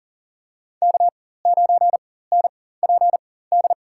Morse code, heard as a keyed tone, about 40 wpm, spaced as for 19 wpm, 700 Hz. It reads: K9NPD
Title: USA Call Signs - 4x Character Spacing - 40wpm